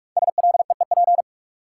Morse code, heard as a keyed tone, about 44 wpm, 700 Hz. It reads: SLEEP
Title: Sentences from Top 500 Words - New Word Drill - 45wpm